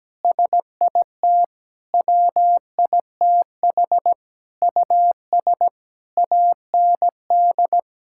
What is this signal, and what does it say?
Morse code, heard as a keyed tone, about 17 wpm, 700 Hz. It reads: SIT WITH US AND
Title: Sentences from Top 1000 Words - Review Part 5 - 17wpm